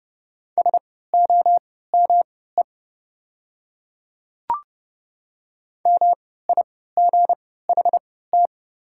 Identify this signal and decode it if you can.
Morse code, heard as a keyed tone, about 30 wpm, spaced as for 17 wpm, 700 Hz. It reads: SOME MIGHT
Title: 3x Character Spacing - Top 300 Words - 30wpm